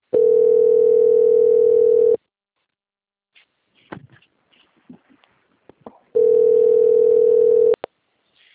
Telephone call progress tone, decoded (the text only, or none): ringback tone